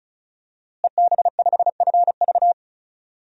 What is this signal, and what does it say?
Morse code, heard as a keyed tone, about 34 wpm, 700 Hz. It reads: EB5FV
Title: DXCC-100 Call Signs - 5 Characters - Repeated 3x - 35wpm